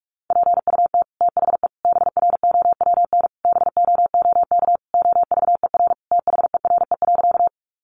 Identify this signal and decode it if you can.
Morse code, heard as a keyed tone, about 44 wpm, 700 Hz. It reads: PUT THE BROWN BOOK OVER THERE.
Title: Sentences from Top 1200 Words 45wpm